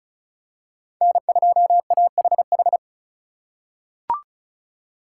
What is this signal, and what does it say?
Morse code, heard as a keyed tone, about 36 wpm, 700 Hz. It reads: N2AHH
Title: USA Call Signs - 2x Character Spacing - 35wpm